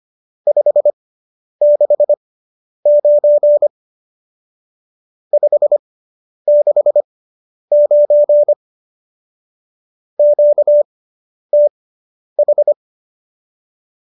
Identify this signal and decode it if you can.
Morse code, heard as a keyed tone, about 25 wpm, spaced as for 10 wpm, 590 Hz. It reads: 569 569 QTH